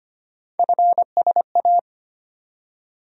Morse code, heard as a keyed tone, about 25 wpm, 700 Hz. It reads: FSA
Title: USA Call Sign Postfixes - 3 Letters 25wpm